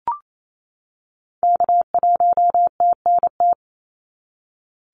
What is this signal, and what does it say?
Morse code, heard as a keyed tone, about 28 wpm, 700 Hz. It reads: K1TNT
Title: NA SOTA Activator Callsigns 28wpm